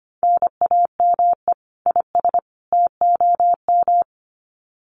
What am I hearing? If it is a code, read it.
Morse code, heard as a keyed tone, about 25 wpm, 700 Hz. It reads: NAME IS TOM
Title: Common QSO Phrases 25wpm